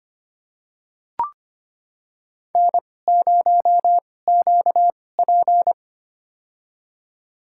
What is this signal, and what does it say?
Morse code, heard as a keyed tone, about 25 wpm, spaced as for 18 wpm, 700 Hz. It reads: N0QP
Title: USA Call Signs - 4 Characters - 2x Character Spacing - 25wpm